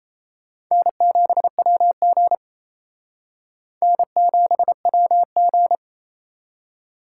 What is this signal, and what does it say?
Morse code, heard as a keyed tone, about 37 wpm, 700 Hz. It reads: N7WG N7WG